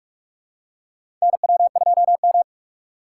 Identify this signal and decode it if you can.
Morse code, heard as a keyed tone, about 43 wpm, 700 Hz. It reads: NW2M
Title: USA Call Signs - 4 Characters - 2x Character Spacing - 45wpm